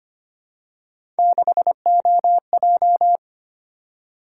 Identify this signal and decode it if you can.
Morse code, heard as a keyed tone, about 25 wpm, 700 Hz. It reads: 6OJ
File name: Three Letter-Number Combinations - 25wpm